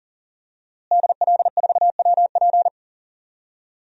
Morse code, heard as a keyed tone, about 40 wpm, 700 Hz. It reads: DL4WP
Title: DXCC-100 Call Signs - 5 Characters - Repeated 2x - 40wpm